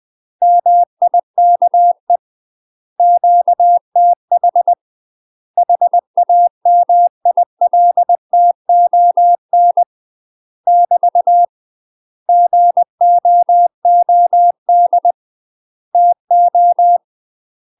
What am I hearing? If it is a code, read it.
Morse code, heard as a keyed tone, about 20 wpm, 700 Hz. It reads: MIKE QTH HAMILTON <BT> GOOD TO
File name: UK MASTER QSOs TRAINING / 20WPM / UK MASTER QSOs 0043